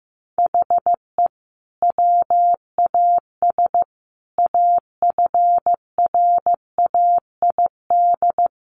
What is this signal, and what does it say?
Morse code, heard as a keyed tone, about 15 wpm, 700 Hz. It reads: HE WAS AFRAID
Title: Sentences from Top 1500 Words 15wpm